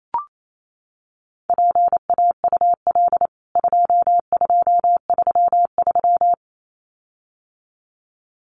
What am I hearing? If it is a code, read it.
Morse code, heard as a keyed tone, about 28 wpm, 700 Hz. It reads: PAUL 2233
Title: CWT Exchange 28wpm